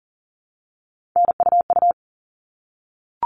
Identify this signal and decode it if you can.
Morse code, heard as a keyed tone, about 40 wpm, 700 Hz. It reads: NUU
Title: All Cut Numbers - 3-Digit 40wpm